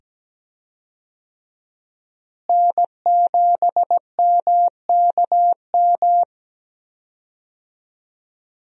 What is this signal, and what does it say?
Morse code, heard as a keyed tone, about 17 wpm, 700 Hz. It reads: N7MKM
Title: USA Call Signs - 5 Characters - Repeated 2x - 17wpm